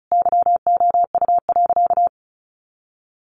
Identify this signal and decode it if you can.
Morse code, heard as a keyed tone, about 34 wpm, 700 Hz. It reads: YOU.
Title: Sentences from Top 200 Words 35wpm